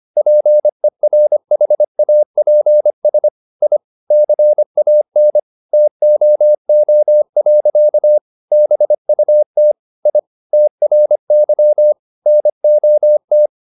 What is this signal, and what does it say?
Morse code, heard as a keyed tone, about 25 wpm, 600 Hz. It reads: PERHAPS I CAN TOO. BUT I TRY NOT
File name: the old man and the sea part 1 25wpm